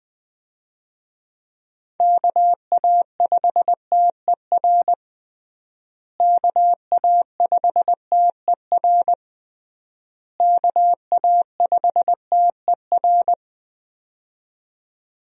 Morse code, heard as a keyed tone, about 20 wpm, 700 Hz. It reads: KA5TER KA5TER KA5TER
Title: USA Call Signs - 6 Characters - Repeated 3x - 20wpm